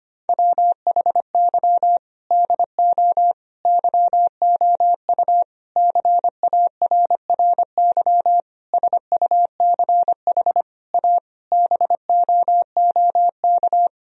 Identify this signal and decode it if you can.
Morse code, heard as a keyed tone, about 25 wpm, 700 Hz. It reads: WHY DO YOU CARRY SUCH A BOOK
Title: Sentences from Top 1000 Words - Review Part 1 - 25wpm